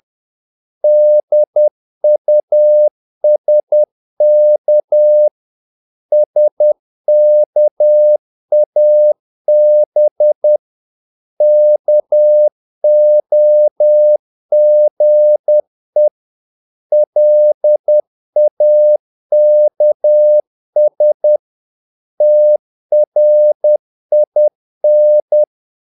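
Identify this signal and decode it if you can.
Morse code, heard as a keyed tone, about 10 wpm, 600 Hz. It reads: DUSK SKAB KOGE LAKS TRIN
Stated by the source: Morse practice of short Danish words at 10 wpm, from CW med Gnister.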